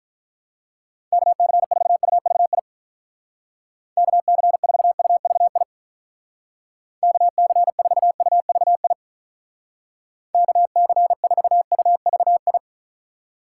Morse code, heard as a keyed tone, about 43 wpm, 700 Hz. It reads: KC4UVI KC4UVI KC4UVI KC4UVI
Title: Speed Racing - USA Call Signs 35wpm